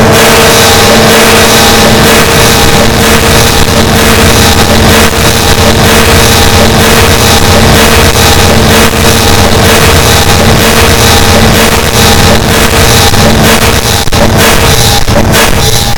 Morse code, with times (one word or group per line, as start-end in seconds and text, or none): none